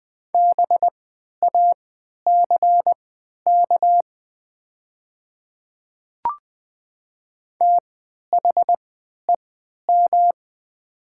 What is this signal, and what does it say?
Morse code, heard as a keyed tone, about 20 wpm, spaced as for 11 wpm, 700 Hz. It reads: BACK THEM
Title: 3x Character Spacing - Top 200 Words - 20wpm